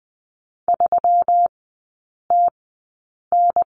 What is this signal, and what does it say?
Morse code, heard as a keyed tone, about 20 wpm, spaced as for 8 wpm, 700 Hz. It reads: 3TN
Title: 138 Lesson 23e - Letter f - Three Characters 20wpm